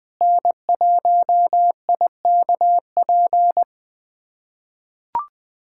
Morse code, heard as a keyed tone, about 21 wpm, 700 Hz. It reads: N1IKP
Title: USA Callsign DE Callsign - 5 Characters - 20wpm